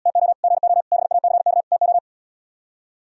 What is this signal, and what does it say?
Morse code, heard as a keyed tone, about 78 wpm, 700 Hz. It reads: TO GO BACK TO